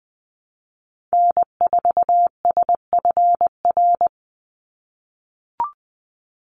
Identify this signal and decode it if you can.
Morse code, heard as a keyed tone, about 20 wpm, 700 Hz. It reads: N4SFR
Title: NA SOTA Activator Callsigns 20wpm